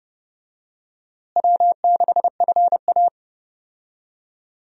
Morse code, heard as a keyed tone, about 32 wpm, 700 Hz. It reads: W6FA
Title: USA Call Signs - 4 Characters - Repeated 3x - 30wpm